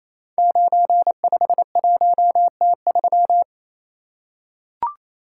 Morse code, heard as a keyed tone, about 27 wpm, 700 Hz. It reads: 951T3
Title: Code Groups - 5 Letter:Numbers - 28wpm